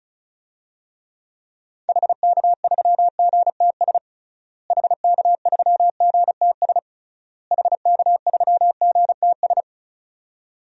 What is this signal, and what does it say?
Morse code, heard as a keyed tone, about 36 wpm, 700 Hz. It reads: HK3GTS HK3GTS HK3GTS
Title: DXCC-100 Call Signs - 6 Characters - Repeated 3x - 35wpm